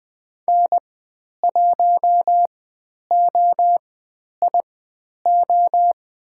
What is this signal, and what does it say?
Morse code, heard as a keyed tone, about 20 wpm, spaced as for 10 wpm, 700 Hz. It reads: N1OIO
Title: 095 Lesson 16g - Letter c - Call Signs 20wpm